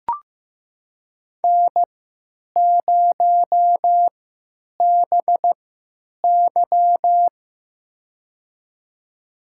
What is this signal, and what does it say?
Morse code, heard as a keyed tone, about 15 wpm, spaced as for 9 wpm, 700 Hz. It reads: N0BY
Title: USA Call Signs - 4 Characters - 3x Character Spacing - 15wpm